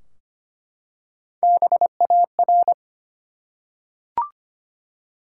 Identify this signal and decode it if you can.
Morse code, heard as a keyed tone, about 25 wpm, 700 Hz. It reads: BAR
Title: Three Letter Words - 25wpm